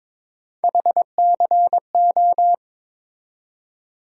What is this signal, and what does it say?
Morse code, heard as a keyed tone, about 22 wpm, 700 Hz. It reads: HCO
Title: USA Call Sign Postfixes - 3 Letters 22wpm